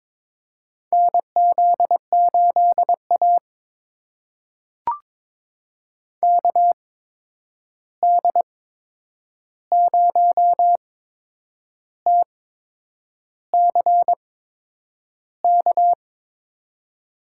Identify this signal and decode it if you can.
Morse code, heard as a keyed tone, about 22 wpm, spaced as for 8 wpm, 700 Hz. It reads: NZ8A KD0TCK
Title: USA Call Signs - 8x Character Spacing - 22wpm